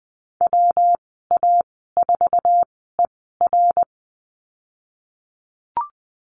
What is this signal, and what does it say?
Morse code, heard as a keyed tone, about 20 wpm, spaced as for 14 wpm, 700 Hz. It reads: WA4ER
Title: 128 Lesson 21g - Number 6 - Call Signs 20wpm